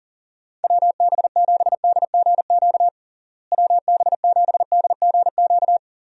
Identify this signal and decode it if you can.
Morse code, heard as a keyed tone, about 40 wpm, 700 Hz. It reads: WB7DGQ WB7DGQ
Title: USA Call Signs - 6 Characters - Repeated 2x - 40wpm